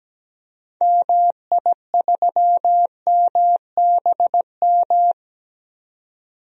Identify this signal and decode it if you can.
Morse code, heard as a keyed tone, about 17 wpm, 700 Hz. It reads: MI3MBM
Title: DXCC-100 Call Signs - 6 Characters - 17wpm